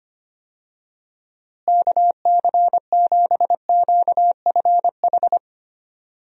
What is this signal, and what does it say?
Morse code, heard as a keyed tone, about 25 wpm, 700 Hz. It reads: KC7QFH
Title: USA Call Signs - 6 Characters - Repeated 3x - 25wpm